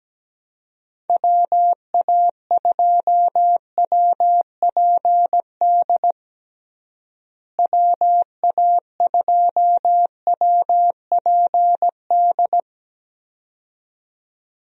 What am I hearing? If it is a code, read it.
Morse code, heard as a keyed tone, about 17 wpm, 700 Hz. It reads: WA2WPD WA2WPD
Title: USA Call Signs - 6 Characters - Repeated 2x - 17wpm